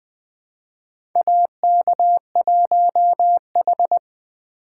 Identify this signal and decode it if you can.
Morse code, heard as a keyed tone, about 20 wpm, 700 Hz. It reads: AK1H
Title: USA Call Signs - 2x Character Spacing - 20wpm